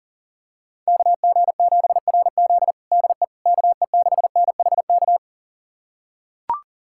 Morse code, heard as a keyed tone, about 40 wpm, 700 Hz. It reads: KG7RZ DE KE6NHK
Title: USA Callsign DE Callsign 40wpm